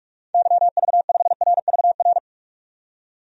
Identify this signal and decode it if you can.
Morse code, heard as a keyed tone, about 41 wpm, 700 Hz. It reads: YV5RVR
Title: DXCC-100 Call Signs - 6 Characters - 45wpm